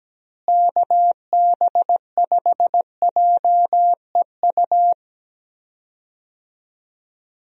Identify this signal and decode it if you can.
Morse code, heard as a keyed tone, about 17 wpm, 700 Hz. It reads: KB5JEU
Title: USA Call Signs - 6 Characters - Repeated 2x - 17wpm